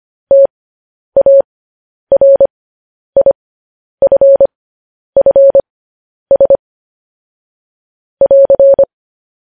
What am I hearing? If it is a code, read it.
Morse code, heard as a keyed tone, about 25 wpm, spaced as for 10 wpm, 550 Hz. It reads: TARIFFS <AR>
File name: podcast / News headlines-1x-25x10